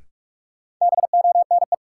Morse code, heard as a keyed tone, about 41 wpm, 700 Hz. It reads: BONE
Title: Sentences from Top 1600 Words - New Word Drill - 45wpm